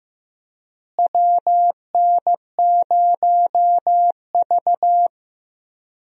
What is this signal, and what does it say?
Morse code, heard as a keyed tone, about 16 wpm, 700 Hz. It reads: WN0V
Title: USA Call Signs - 4 Characters - Repeated 2x - 15wpm